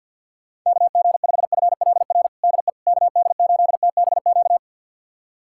Morse code, heard as a keyed tone, about 45 wpm, 700 Hz. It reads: KG5FLR DE KD8TBQ
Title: USA Callsign DE Callsign 50wpm